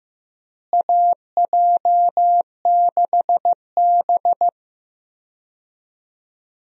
Morse code, heard as a keyed tone, about 15 wpm, 700 Hz. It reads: AJ6B
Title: USA Call Signs - 4 Characters - Repeated 3x - 15wpm